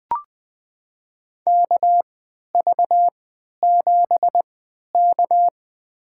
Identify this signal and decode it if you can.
Morse code, heard as a keyed tone, about 20 wpm, spaced as for 11 wpm, 700 Hz. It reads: KV7K
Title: USA Call Signs - 4 Characters - 3x Character Spacing - 20wpm